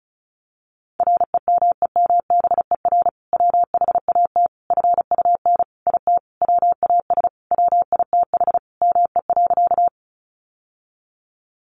Morse code, heard as a keyed tone, about 34 wpm, 700 Hz. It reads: REMEMBER WHAT FUN IT WAS WITH ME.
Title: Sentences from Top 700 Words 35wpm